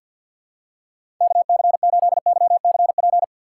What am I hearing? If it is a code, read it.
Morse code, heard as a keyed tone, about 49 wpm, 690 Hz. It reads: KC8YCP
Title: USA Call Signs - 6 Characters - 3x Character Spacing - 50wpm